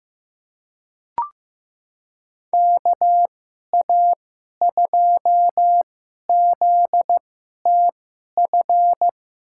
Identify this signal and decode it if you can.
Morse code, heard as a keyed tone, about 16 wpm, spaced as for 11 wpm, 700 Hz. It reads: KA2ZTF
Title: USA Call Signs - 6 Characters - 2x Character Spacing - 15wpm